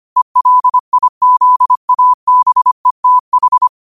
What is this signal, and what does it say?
Morse code, heard as a keyed tone, about 25 wpm, 1000 Hz. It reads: ELIZABETH